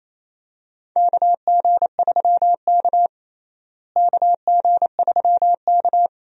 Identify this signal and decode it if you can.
Morse code, heard as a keyed tone, about 28 wpm, 700 Hz. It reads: KG3K KG3K
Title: USA Call Signs - 4 Characters - Repeated 2x - 28wpm